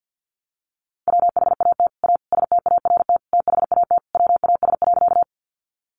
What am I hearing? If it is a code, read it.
Morse code, heard as a keyed tone, about 50 wpm, 700 Hz. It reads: WHAT A START THAT WAS.
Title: Sentences from Top 300 Words 50wpm